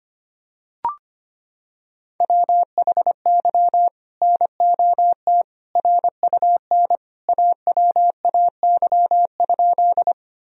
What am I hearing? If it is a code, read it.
Morse code, heard as a keyed tone, about 25 wpm, 700 Hz. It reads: WHY NOT RUN AWAY?
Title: Sentences from Top 1000 Words - Review Part 1 - 25wpm